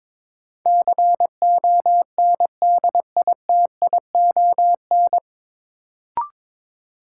Morse code, heard as a keyed tone, about 22 wpm, 700 Hz. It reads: CONDITION
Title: Sentences from Top 1000 Words - New Word Drill - 22wpm